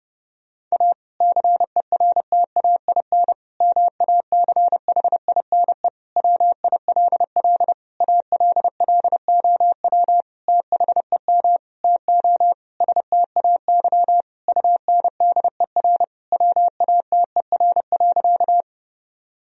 Morse code, heard as a keyed tone, about 30 wpm, 700 Hz. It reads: A CERTAIN MACHINE WILL ALLOW THEM TO STAY UNDER WATER.
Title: Sentences from Top 1000 Words - Review Part 2 - 30wpm